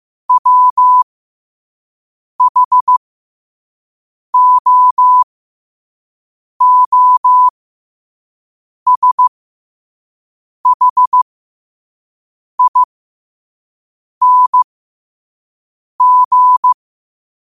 Morse code, heard as a keyed tone, about 15 wpm, spaced as for 5 wpm, 1000 Hz. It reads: WHOOSHING
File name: qotdfiles / 5wpm